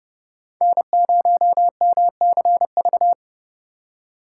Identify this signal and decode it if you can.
Morse code, heard as a keyed tone, about 30 wpm, 700 Hz. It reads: N0MCV
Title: USA Call Signs - 2x Character Spacing - 30wpm